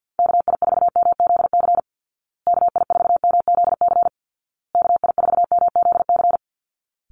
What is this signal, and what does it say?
Morse code, heard as a keyed tone, about 50 wpm, 700 Hz. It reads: KI4MZC KI4MZC KI4MZC
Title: Call Signs Repeated 2x - 50wpm